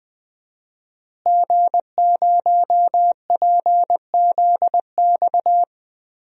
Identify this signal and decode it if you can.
Morse code, heard as a keyed tone, about 20 wpm, 700 Hz. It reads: G0PZX
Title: DXCC-100 Call Signs - 5 Characters - Repeated 3x - 20wpm